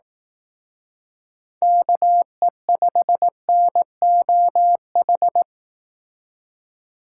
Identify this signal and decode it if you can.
Morse code, heard as a keyed tone, about 18 wpm, 700 Hz. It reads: KE5NOH
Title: Speed Racing - USA 2x3 Call Signs 15wpm